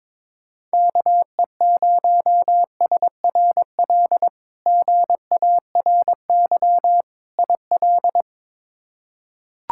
Morse code, heard as a keyed tone, about 21 wpm, 700 Hz. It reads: KE0SRL GARY IL
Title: K1USN SST QSO Practice - 22wpm